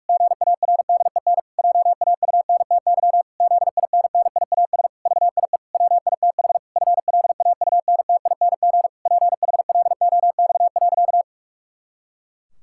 Morse code, heard as a keyed tone, about 46 wpm, 700 Hz. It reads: GARDEN JAUNTY ZINNIAS VIE WITH FLAUNTING PHLOX.
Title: Pangram Challenge v2 - 45wpm